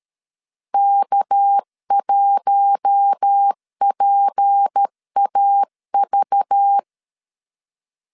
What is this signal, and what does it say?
Morse code, heard as a keyed tone, about 13 wpm, 800 Hz. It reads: K1PAV
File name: CW / DET / 1 6 1500C S C 12wpm 3533